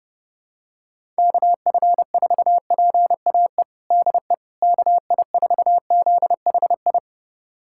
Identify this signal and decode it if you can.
Morse code, heard as a keyed tone, about 30 wpm, 700 Hz. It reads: KF4PAE DE KI4ZHI
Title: USA Callsign DE Callsign - 6 Characters - 30wpm